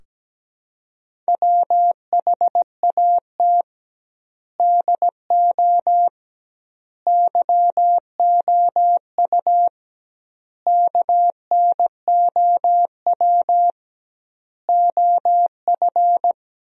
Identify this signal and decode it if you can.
Morse code, heard as a keyed tone, about 17 wpm, 700 Hz. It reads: WHAT DO YOU KNOW OF
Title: 2x Word-Spacing - Sentences from Top 100 Words - 17wpm